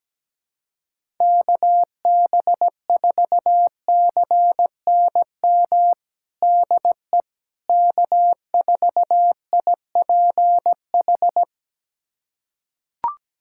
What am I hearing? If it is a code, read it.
Morse code, heard as a keyed tone, about 17 wpm, 700 Hz. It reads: KB4CNM DE K4IPH
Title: USA Callsign DE Callsign 17wpm